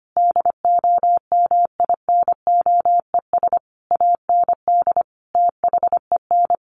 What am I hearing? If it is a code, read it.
Morse code, heard as a keyed tone, about 25 wpm, 700 Hz. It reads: DOMINOES AND THEN
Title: A Sound Of Thunder 25wpm